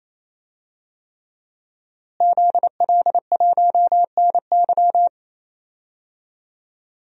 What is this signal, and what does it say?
Morse code, heard as a keyed tone, about 27 wpm, 700 Hz. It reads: ZL1NY
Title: DXCC-100 Call Signs - 5 Characters - Repeated 3x - 28wpm